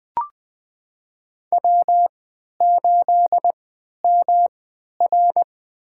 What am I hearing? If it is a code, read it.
Morse code, heard as a keyed tone, about 20 wpm, spaced as for 11 wpm, 700 Hz. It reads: W8MR
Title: USA Call Signs - 4 Characters - 3x Character Spacing - 20wpm